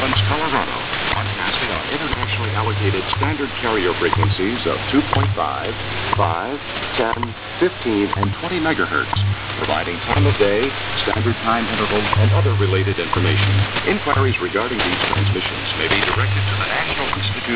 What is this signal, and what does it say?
Radio time signal, with speech.
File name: luxorion / Radio / wwv-nist-5mhz-0900gmt